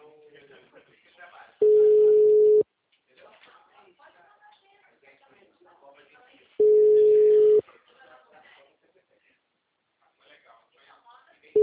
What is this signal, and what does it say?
Ringback tone (Europe, 425 Hz), with speech.
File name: out- -2016- - - .6920